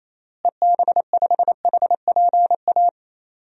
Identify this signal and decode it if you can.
Morse code, heard as a keyed tone, about 28 wpm, 700 Hz. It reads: EB5HPA
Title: DXCC-100 Call Signs - 6 Characters - Repeated 3x - 28wpm